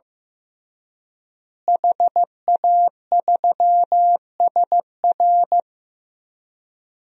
Morse code, heard as a keyed tone, about 15 wpm, 700 Hz. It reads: HA3SR
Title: DXCC-100 Call Signs - 5 Characters - Repeated 3x - 15wpm